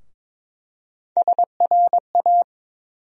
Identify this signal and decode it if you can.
Morse code, heard as a keyed tone, about 22 wpm, 700 Hz. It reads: SRA
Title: USA Call Sign Postfixes - 3 Letters 22wpm